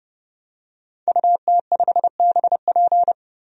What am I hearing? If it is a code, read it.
Morse code, heard as a keyed tone, about 31 wpm, 700 Hz. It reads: UT5BP
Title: DXCC-100 Call Signs - 5 Characters - 30wpm